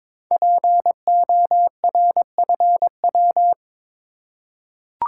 Morse code, heard as a keyed tone, about 22 wpm, 700 Hz. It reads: PORFW
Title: Code Groups - 5 Letters - 22wpm